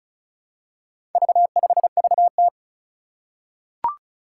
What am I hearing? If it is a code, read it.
Morse code, heard as a keyed tone, about 36 wpm, 700 Hz. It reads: V5VT
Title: Four Challenging Letter-Number Combinations 35wpm